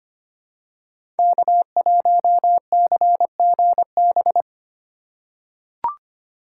Morse code, heard as a keyed tone, about 25 wpm, 700 Hz. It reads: K1CGB
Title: Code Groups - 5 Letter:Numbers - 25wpm